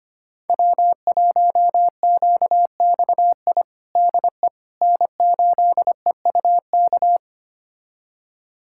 Morse code, heard as a keyed tone, about 25 wpm, 700 Hz. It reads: W1QXI DE N8EUK
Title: USA Callsign DE Callsign - 5 Characters - 25wpm